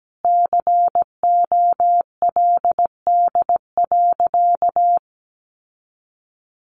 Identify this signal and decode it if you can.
Morse code, heard as a keyed tone, about 17 wpm, 700 Hz. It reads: COLD.
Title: Sentences from Top 800 Words 17wpm